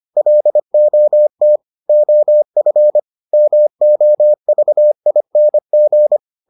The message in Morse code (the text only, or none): LOT OF MOVING